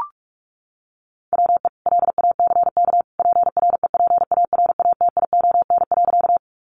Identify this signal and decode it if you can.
Morse code, heard as a keyed tone, about 44 wpm, 700 Hz. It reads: WE LACK PREPARATION.
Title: Sentences from Top 1400 Words 45wpm